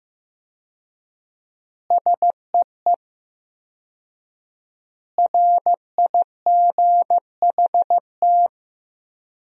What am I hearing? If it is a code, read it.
Morse code, heard as a keyed tone, about 16 wpm, 700 Hz. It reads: SEE RIGHT
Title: 4x Word-Spacing - Sentences from Top 200 Words - 15wpm